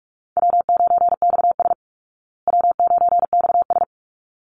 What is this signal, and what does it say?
Morse code, heard as a keyed tone, about 44 wpm, 700 Hz. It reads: W9XS W9XS
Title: Easy Call Signs Repeated 2x - 45wpm